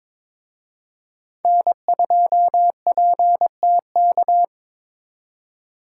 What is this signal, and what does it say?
Morse code, heard as a keyed tone, about 27 wpm, 700 Hz. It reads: N2PTK
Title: Speed Racing - USA 1x3 Call Signs 22wpm